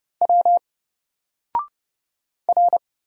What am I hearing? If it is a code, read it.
Morse code, heard as a keyed tone, about 30 wpm, 700 Hz. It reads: W R
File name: Single-Letter-Number-30wpm